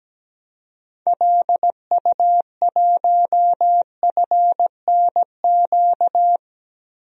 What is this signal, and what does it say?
Morse code, heard as a keyed tone, about 17 wpm, 700 Hz. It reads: LU1FNQ
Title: DXCC-100 Call Signs - 6 Characters - Repeated 2x - 17wpm